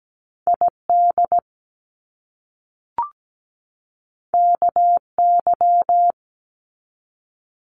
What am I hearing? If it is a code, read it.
Morse code, heard as a keyed tone, about 17 wpm, 700 Hz. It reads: ID KY
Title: US States 17wpm